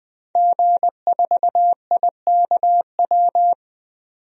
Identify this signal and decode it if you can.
Morse code, heard as a keyed tone, about 21 wpm, 700 Hz. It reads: G4IKW
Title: DXCC-100 Call Signs - 5 Characters - 20wpm